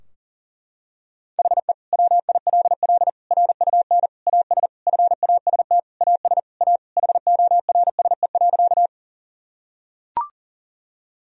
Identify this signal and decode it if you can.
Morse code, heard as a keyed tone, about 40 wpm, 700 Hz. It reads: HE WILL RUN AS FAST AS A HORSE.
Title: Sentences from Top 1000 Words - Review Part 2 - 40wpm